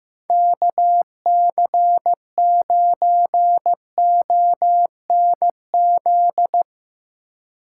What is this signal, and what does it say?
Morse code, heard as a keyed tone, about 15 wpm, 700 Hz. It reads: KC9ONZ
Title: Speed Racing - USA 2x3 Call Signs 15wpm